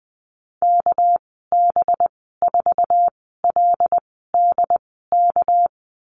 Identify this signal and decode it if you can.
Morse code, heard as a keyed tone, about 20 wpm, spaced as for 15 wpm, 700 Hz. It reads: KB4LDK
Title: 207 Lesson 33g - Letter j - Call Signs 20wpm